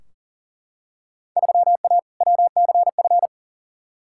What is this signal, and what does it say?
Morse code, heard as a keyed tone, about 40 wpm, 700 Hz. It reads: 3A WCF
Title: Field Day Exchanges Encore - 40wpm